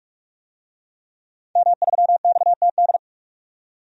Morse code, heard as a keyed tone, about 46 wpm, 700 Hz. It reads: M3XTD
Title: DXCC-100 Call Signs - 5 Characters - 45wpm